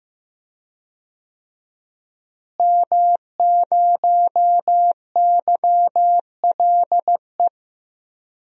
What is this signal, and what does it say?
Morse code, heard as a keyed tone, about 15 wpm, 700 Hz. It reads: M0YLE
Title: DXCC-100 Call Signs - 5 Characters - Repeated 2x - 15wpm